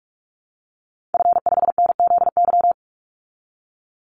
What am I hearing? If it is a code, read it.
Morse code, heard as a keyed tone, about 44 wpm, 700 Hz. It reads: F5NZY
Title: Call Signs Repeated 3x - 45wpm